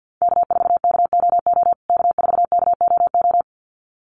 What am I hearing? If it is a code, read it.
Morse code, heard as a keyed tone, about 50 wpm, 700 Hz. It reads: K4KOO K4KOO
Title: USA Call Signs Repeated 2x 50wpm